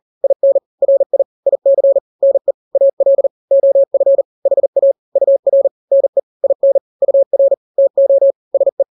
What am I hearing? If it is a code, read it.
Morse code, heard as a keyed tone, about 40 wpm, 550 Hz. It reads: IN RI IC NE AL OF HA UR NE IN UR TO SE
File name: n-grams / 2-grams 40 35 WPM